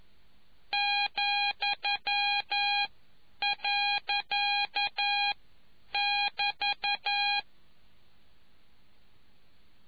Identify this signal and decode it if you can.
Morse code, heard as a keyed tone, about 11 wpm, 820 Hz. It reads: , . <BT>